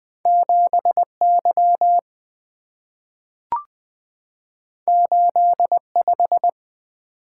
Morse code, heard as a keyed tone, about 20 wpm, 700 Hz. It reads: 7Y 85
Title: Two Letter-Number Combinations - 20wpm